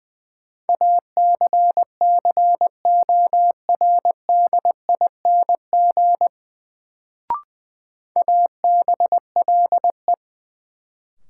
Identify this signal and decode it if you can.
Morse code, heard as a keyed tone, about 20 wpm, 700 Hz. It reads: ACCORDING ABLE
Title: Sets of 2 Words - New Word Drill - 20wpm